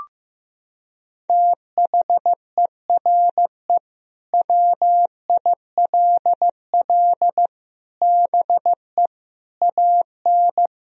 Morse code, heard as a keyed tone, about 16 wpm, 700 Hz. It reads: THERE WILL BE AN
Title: Sentences from Top 1000 Words - Review Part 4 - 15wpm